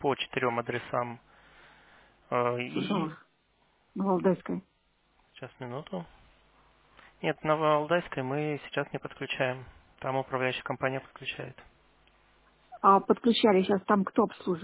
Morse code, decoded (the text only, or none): none